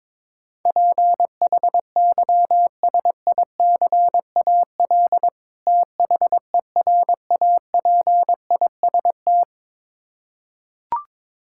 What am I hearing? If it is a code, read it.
Morse code, heard as a keyed tone, about 22 wpm, 700 Hz. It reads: PHYSICAL THERAPIST
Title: Practice QSO - Exchange 3 - More Info - 22wpm